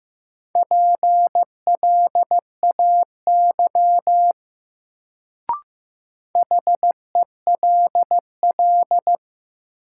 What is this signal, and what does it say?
Morse code, heard as a keyed tone, about 15 wpm, 700 Hz. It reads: PLAY HELL
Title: Sets of 5 Words Encore - New Word Drill - 15wpm